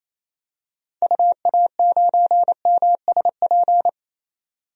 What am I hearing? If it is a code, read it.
Morse code, heard as a keyed tone, about 28 wpm, 700 Hz. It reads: UA9MSP
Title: DXCC-100 Call Signs - 6 Characters - 28wpm